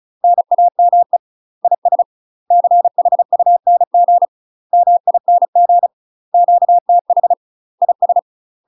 Morse code, heard as a keyed tone, about 36 wpm, 700 Hz. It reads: NAME IS CHUNG MING QTH IS
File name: UK MASTER QSOs TRAINING / 35WPM / UK MASTER QSOs 0041